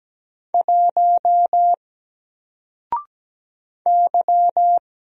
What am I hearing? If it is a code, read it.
Morse code, heard as a keyed tone, about 17 wpm, 700 Hz. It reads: 1 Y
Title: Single Character (Letter:Number:Punctuation:Prosign) 17wpm